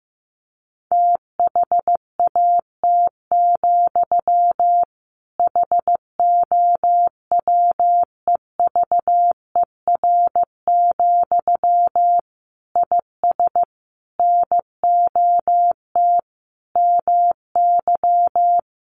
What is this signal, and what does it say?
Morse code, heard as a keyed tone, about 15 wpm, 700 Hz. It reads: THAT, HOWEVER, IS NOT MY
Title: Sentences from Top 1500 Words 15wpm